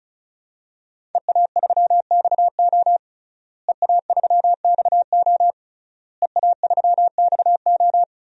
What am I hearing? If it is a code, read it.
Morse code, heard as a keyed tone, about 34 wpm, 700 Hz. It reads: EA3XO EA3XO EA3XO
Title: DXCC-100 Call Signs - 5 Characters - Repeated 3x - 35wpm